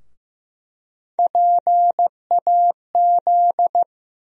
Morse code, heard as a keyed tone, about 15 wpm, 700 Hz. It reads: PAZ
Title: USA Call Sign Postfixes - 3 Letters 15wpm